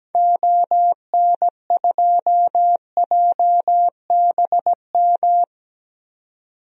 Morse code, heard as a keyed tone, about 17 wpm, 700 Hz. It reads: ON2JBM
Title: DXCC-100 Call Signs - 6 Characters - 17wpm